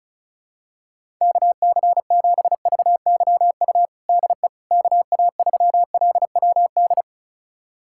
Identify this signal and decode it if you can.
Morse code, heard as a keyed tone, about 36 wpm, 700 Hz. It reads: KC7VYU DE KA3LWD
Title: USA Callsign DE Callsign 35wpm